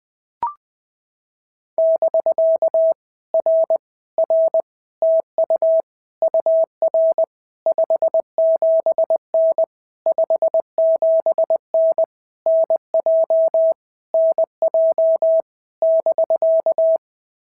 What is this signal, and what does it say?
Morse code, heard as a keyed tone, about 20 wpm, 650 Hz. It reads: <BK> R R TU UR 57N 57N NJ NJ <BK>
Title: POTA QSO Practice - 20wpm